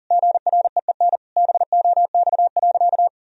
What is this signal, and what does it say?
Morse code, heard as a keyed tone, about 40 wpm, 700 Hz. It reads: GREEN BOX.
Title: Sentences from Top 1000 Words - Review Part 2 - 40wpm